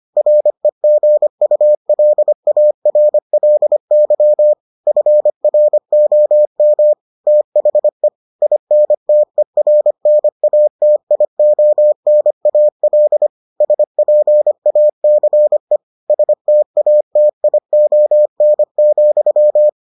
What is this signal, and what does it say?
Morse code, heard as a keyed tone, about 25 wpm, 600 Hz. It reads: REGULARLY FROM THE INTERNATIONAL SPACE STATION,